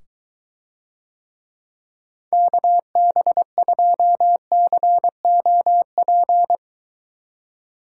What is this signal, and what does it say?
Morse code, heard as a keyed tone, about 23 wpm, 700 Hz. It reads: KB2COP